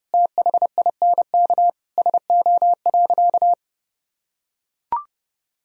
Morse code, heard as a keyed tone, about 30 wpm, 700 Hz. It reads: THINK SO.
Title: Sentences from Top 1000 Words - Review Part 1 - 30wpm